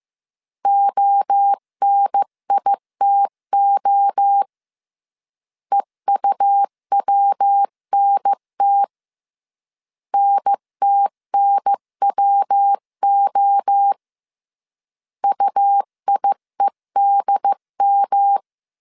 Morse code, heard as a keyed tone, about 15 wpm, 800 Hz. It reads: ONITO EUWNT NTNWO UIEDM
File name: CW / DET / 1 3 1500C S A 14wpm 2447